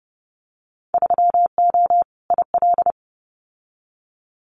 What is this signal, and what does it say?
Morse code, heard as a keyed tone, about 30 wpm, 700 Hz. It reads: 3O IL
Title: Winter Field Day Exchanges 30wpm